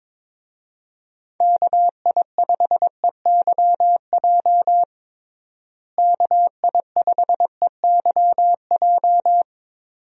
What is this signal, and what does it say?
Morse code, heard as a keyed tone, about 22 wpm, 700 Hz. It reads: KI5EYJ KI5EYJ
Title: USA Call Signs - 6 Characters - Repeated 2x - 22wpm